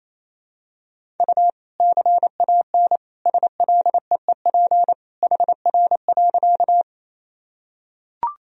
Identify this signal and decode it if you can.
Morse code, heard as a keyed tone, about 28 wpm, 700 Hz. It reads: U CAN SLEEP HR.
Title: Sentences from Top 500 Words and Abbreviations 28wpm